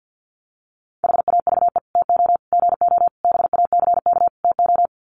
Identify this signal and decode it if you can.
Morse code, heard as a keyed tone, about 50 wpm, 700 Hz. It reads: HAVE TO GO BACK TO
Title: Sets of 5 Words Encore2 50wpm